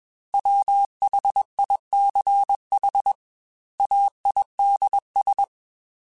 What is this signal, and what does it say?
Morse code, heard as a keyed tone, about 21 wpm, 800 Hz. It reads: WHICH AIDS
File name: Technical Subjects / Technical Subjects-mp3s / XC @20